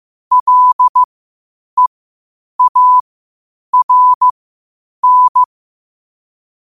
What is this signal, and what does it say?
Morse code, heard as a keyed tone, about 15 wpm, spaced as for 8 wpm, 1000 Hz. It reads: LEARN
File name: qotdfiles / 8wpm